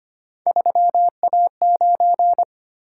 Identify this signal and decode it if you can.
Morse code, heard as a keyed tone, about 25 wpm, 700 Hz. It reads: 3A9
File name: Three Letter-Number Combinations - 25wpm